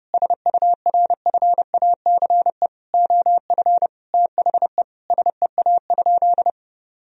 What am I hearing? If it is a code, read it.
Morse code, heard as a keyed tone, about 32 wpm, 700 Hz. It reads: SURFACE OF THE SEA?
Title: Sentences from Top 1000 Words - Review Part 3 - 30wpm